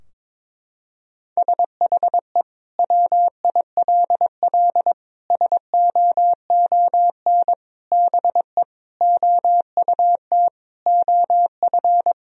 Morse code, heard as a keyed tone, about 22 wpm, 700 Hz. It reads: SHE WILL SOON BE OUT OF
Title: Sentences from Top 1000 Words - Review Part 2 - 22wpm